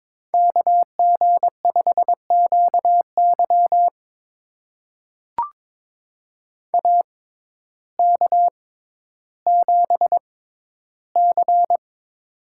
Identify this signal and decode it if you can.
Morse code, heard as a keyed tone, about 21 wpm, spaced as for 12 wpm, 700 Hz. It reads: KG5QY AK7C
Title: USA Call Signs - 6x Character Spacing - 22wpm